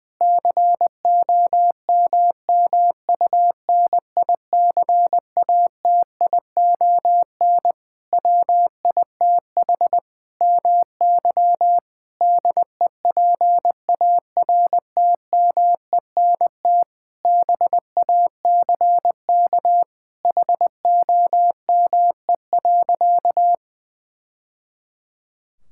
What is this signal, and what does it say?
Morse code, heard as a keyed tone, about 20 wpm, 700 Hz. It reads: COMMUNICATION WITH MY DEPARTMENT BACK HOME.
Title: Sentences from Top 1000 Words - Review Part 6 - 20wpm